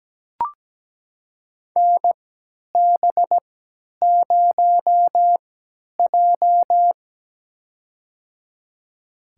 Morse code, heard as a keyed tone, about 17 wpm, spaced as for 10 wpm, 700 Hz. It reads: NB0J
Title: USA Call Signs - 4 Characters - 3x Character Spacing - 17wpm